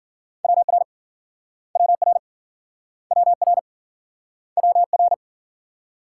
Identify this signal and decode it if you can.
Morse code, heard as a keyed tone, about 53 wpm, 700 Hz. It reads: WR WR WR WR